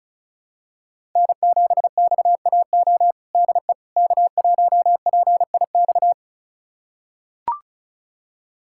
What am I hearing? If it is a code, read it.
Morse code, heard as a keyed tone, about 36 wpm, 700 Hz. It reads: N7XAO DE K1PIX
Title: USA Callsign DE Callsign - 5 Characters - 35wpm